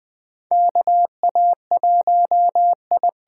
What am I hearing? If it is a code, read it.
Morse code, heard as a keyed tone, about 20 wpm, 700 Hz. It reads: KA1I
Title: USA Call Signs - 6 Characters - 8x Character Spacing - 20wpm